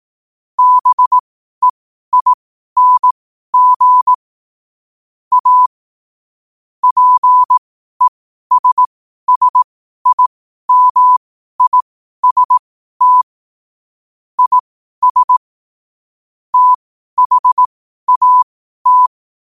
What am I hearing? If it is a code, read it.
Morse code, heard as a keyed tone, about 18 wpm, spaced as for 12 wpm, 1000 Hz. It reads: BEING A PESSIMIST IS THAT